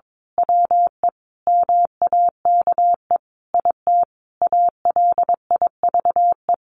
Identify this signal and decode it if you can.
Morse code, heard as a keyed tone, about 22 wpm, 700 Hz. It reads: WE MAKE IT ALIVE
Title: A Sound Of Thunder 22wpm